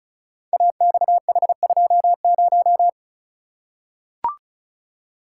Morse code, heard as a keyed tone, about 36 wpm, 700 Hz. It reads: AXH20
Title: Code Groups - 5 Letter:Numbers - 35wpm